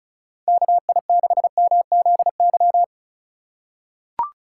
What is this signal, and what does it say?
Morse code, heard as a keyed tone, about 36 wpm, 700 Hz. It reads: KI6MZY
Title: USA Call Signs - 6 Characters - 3x Character Spacing - 35wpm